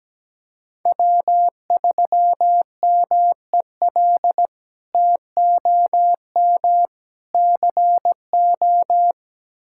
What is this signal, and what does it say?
Morse code, heard as a keyed tone, about 17 wpm, 700 Hz. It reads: W3MEL TOM CO
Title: K1USN SST QSO Practice - 17wpm